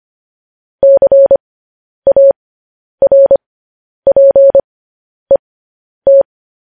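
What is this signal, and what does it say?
Morse code, heard as a keyed tone, about 25 wpm, spaced as for 10 wpm, 550 Hz. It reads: CARPET
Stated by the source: News headlines sent in Morse code at 10 wpm.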